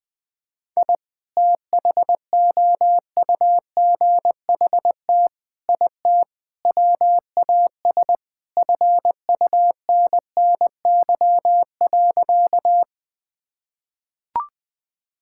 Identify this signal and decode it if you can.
Morse code, heard as a keyed tone, about 20 wpm, 700 Hz. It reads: I THOUGHT IT WAS FUNNY.
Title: Sentences with 4-Word Phrases - Spoken Beginning - 20wpm